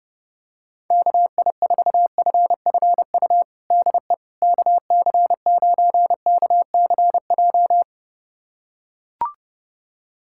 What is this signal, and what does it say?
Morse code, heard as a keyed tone, about 30 wpm, 700 Hz. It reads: KI4FFU DE KC9KCJ
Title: USA Callsign DE Callsign 30wpm